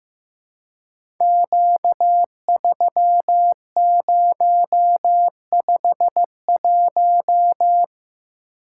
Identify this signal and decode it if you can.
Morse code, heard as a keyed tone, about 15 wpm, 700 Hz. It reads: Q3051
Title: Code Groups - 5 Letter:Numbers - 15wpm